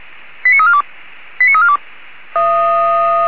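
Five-tone selective call (ZVEI), heard as quick runs of five tones, tones of about 70 ms. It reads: 89342 89342
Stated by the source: Five-tone ZVEI selective call, received by SDR, fire brigade Orlamünde.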